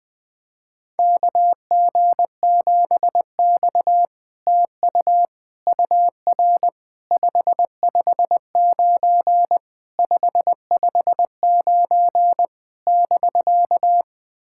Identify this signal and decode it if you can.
Morse code, heard as a keyed tone, about 20 wpm, 700 Hz. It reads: KG7X TU UR 559 559 <BK>
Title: SOTA QSO Practice 20wpm